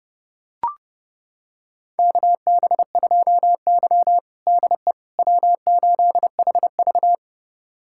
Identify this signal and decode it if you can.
Morse code, heard as a keyed tone, about 30 wpm, 700 Hz. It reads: KB2Y DE W8HV
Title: USA Callsign DE Callsign - 4 Characters - 30wpm